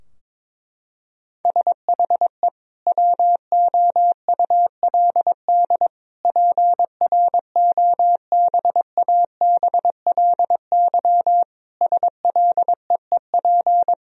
Morse code, heard as a keyed tone, about 22 wpm, 700 Hz. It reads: SHE WOULD PROBABLY SLEEP
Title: Sentences from Top 1000 Words - Review Part 4 - 22wpm